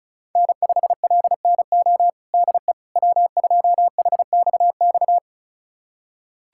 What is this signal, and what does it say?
Morse code, heard as a keyed tone, about 34 wpm, 700 Hz. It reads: N5LNO DE W2HXX
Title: USA Callsign DE Callsign - 5 Characters - 35wpm